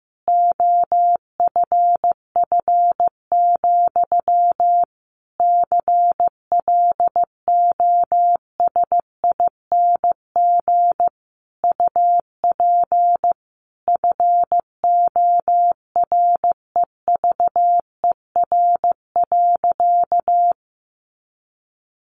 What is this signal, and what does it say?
Morse code, heard as a keyed tone, about 15 wpm, 700 Hz. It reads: OFF, CLOSING UP FOREVER.
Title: A Sound Of Thunder 15wpm